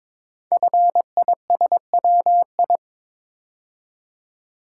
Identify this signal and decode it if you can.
Morse code, heard as a keyed tone, about 22 wpm, 700 Hz. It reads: FISWI
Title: Code Groups - 5 Letters - 22wpm